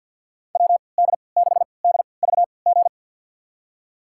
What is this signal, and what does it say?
Morse code, heard as a keyed tone, about 48 wpm, spaced as for 28 wpm, 700 Hz. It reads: WD6DVG
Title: USA Call Signs - 6 Characters - 3x Character Spacing - 50wpm